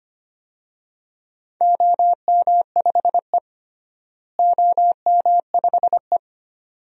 Morse code, heard as a keyed tone, about 25 wpm, 700 Hz. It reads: OM5E OM5E
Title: DXCC-100 Call Signs - 4 Characters - Repeated 2x - 25wpm